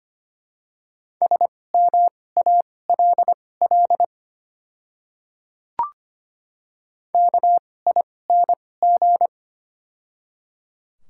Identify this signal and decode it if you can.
Morse code, heard as a keyed tone, about 25 wpm, spaced as for 18 wpm, 700 Hz. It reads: SMALL KING
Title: 2x Character Spacing - Top 500 Words - 25wpm